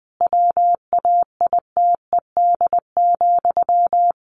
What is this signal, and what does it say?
Morse code, heard as a keyed tone, about 20 wpm, 700 Hz. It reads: WAITED,
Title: A Sound Of Thunder 20wpm